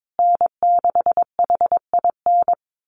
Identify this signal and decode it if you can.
Morse code, heard as a keyed tone, about 22 wpm, 700 Hz. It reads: N6HIN
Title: USA Call Signs Repeated 2x 22wpm